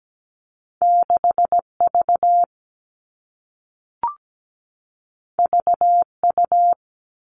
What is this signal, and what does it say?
Morse code, heard as a keyed tone, about 17 wpm, 700 Hz. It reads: 6V VU
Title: All Cut Numbers - 2-Digit 17wpm